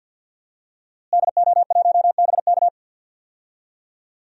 Morse code, heard as a keyed tone, about 47 wpm, 700 Hz. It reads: DO1BK
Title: DXCC-100 Call Signs - 5 Characters - Repeated 3x - 50wpm